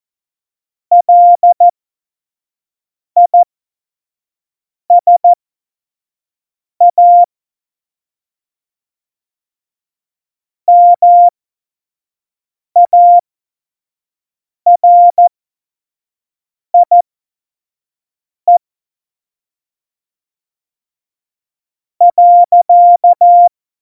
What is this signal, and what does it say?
Morse code, heard as a keyed tone, about 14 wpm, spaced as for 5 wpm, 700 Hz. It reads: LISA MARIE .
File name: Morsekurs / Witze / Witz 01. Dezember, 2020 25 F70